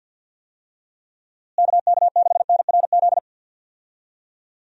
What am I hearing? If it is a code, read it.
Morse code, heard as a keyed tone, about 47 wpm, 700 Hz. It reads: KK6NRZ
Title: USA Call Signs - 6 Characters - Repeated 2x - 50wpm